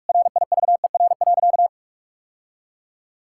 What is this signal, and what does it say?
Morse code, heard as a keyed tone, about 46 wpm, 700 Hz. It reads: RIVER.
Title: Sentences from Top 1000 Words - Review Part 5 - 45wpm